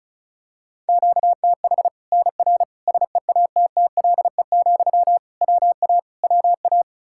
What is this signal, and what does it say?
Morse code, heard as a keyed tone, about 34 wpm, 700 Hz. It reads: QTH NR SEATTLE, WA WA
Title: Practice QSO - Exchange 1 - Beginning - 35wpm